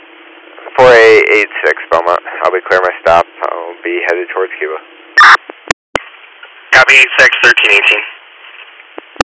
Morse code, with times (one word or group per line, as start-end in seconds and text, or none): none